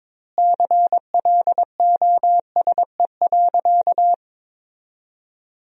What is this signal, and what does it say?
Morse code, heard as a keyed tone, about 22 wpm, 700 Hz. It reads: CLOSE.
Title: 2x Word-Spacing - Sentences from Top 300 Words - 22wpm